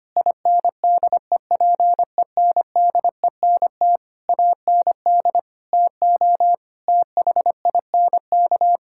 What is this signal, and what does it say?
Morse code, heard as a keyed tone, about 25 wpm, 700 Hz. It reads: INDEPENDENT AND TO THINK
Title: Sentences from Top 1000 Words - Review Part 6 - 25wpm